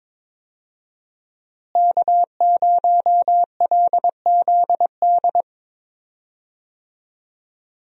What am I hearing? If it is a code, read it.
Morse code, heard as a keyed tone, about 22 wpm, 700 Hz. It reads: K0LZD
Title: USA Call Signs - 5 Characters - Repeated 3x - 22wpm